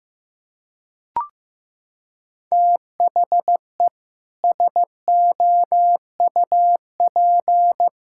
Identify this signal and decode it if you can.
Morse code, heard as a keyed tone, about 15 wpm, 700 Hz. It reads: THE SOUP
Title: Sentences from Top 1000 Words - Review Part 6 - 15wpm